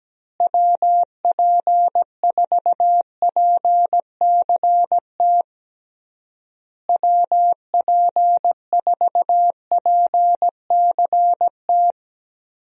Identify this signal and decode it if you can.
Morse code, heard as a keyed tone, about 17 wpm, 700 Hz. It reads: WP4PCT WP4PCT
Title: USA Call Signs - 6 Characters - Repeated 3x - 17wpm